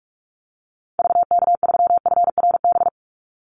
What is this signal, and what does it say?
Morse code, heard as a keyed tone, about 44 wpm, 700 Hz. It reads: VK3FRB
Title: USA Call Signs 45wpm